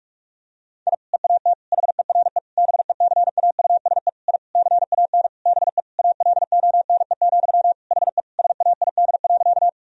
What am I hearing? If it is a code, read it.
Morse code, heard as a keyed tone, about 41 wpm, 700 Hz. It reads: I EAT HERE BECAUSE I CAN BE ALONE, HE SAID.
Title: Sentences from Top 1000 Words - Review Part 3 - 45wpm